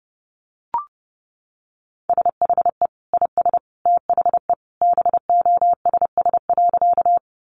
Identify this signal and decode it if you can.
Morse code, heard as a keyed tone, about 30 wpm, 700 Hz. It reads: SHE IS THE BOSS.
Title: Sentences from Top 700 Words 30wpm